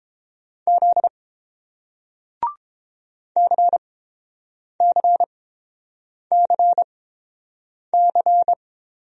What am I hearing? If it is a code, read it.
Morse code, heard as a keyed tone, about 30 wpm, spaced as for 7 wpm, 700 Hz. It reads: Z CCCC